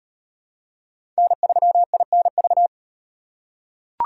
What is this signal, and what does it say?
Morse code, heard as a keyed tone, about 38 wpm, 700 Hz. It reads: N3INV